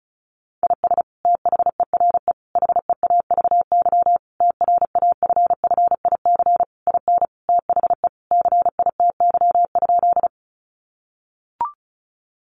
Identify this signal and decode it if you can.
Morse code, heard as a keyed tone, about 34 wpm, 700 Hz. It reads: IS THERE HEAVY TRAFFIC IN THE CITY?
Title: Sentences from Top 1300 Words 35wpm